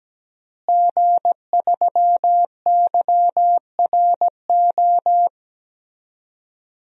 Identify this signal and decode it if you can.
Morse code, heard as a keyed tone, about 17 wpm, 700 Hz. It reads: G3YRO
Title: DXCC-100 Call Signs - 5 Characters - Repeated 3x - 17wpm